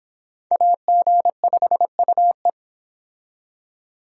Morse code, heard as a keyed tone, about 25 wpm, 700 Hz. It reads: AG5UE